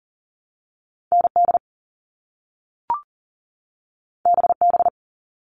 Morse code, heard as a keyed tone, about 38 wpm, 700 Hz. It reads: ND BB
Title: All Cut Numbers - 2-Digit 40wpm